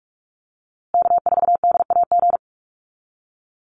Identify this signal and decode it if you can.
Morse code, heard as a keyed tone, about 44 wpm, 700 Hz. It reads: K4DAG
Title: USA Call Signs Repeated 3x 45wpm